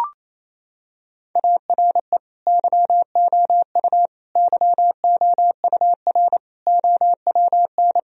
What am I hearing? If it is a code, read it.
Morse code, heard as a keyed tone, about 28 wpm, 700 Hz. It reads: ARE YOU YOUR OWN
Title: Sentences from Top 1000 Words - Review Part 4 - 28wpm